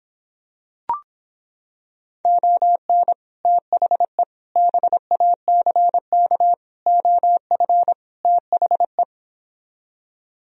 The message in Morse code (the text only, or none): ON THE BACK OF THE